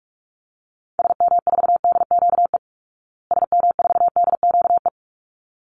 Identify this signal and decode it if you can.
Morse code, heard as a keyed tone, about 44 wpm, 700 Hz. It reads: SM4DQE SM4DQE
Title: Call Signs Repeated 3x - 45wpm